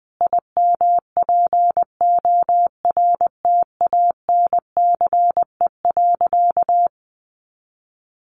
Morse code, heard as a keyed tone, about 20 wpm, 700 Hz. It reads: IMPORTANCE.
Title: Sentences from Top 1200 Words 20wpm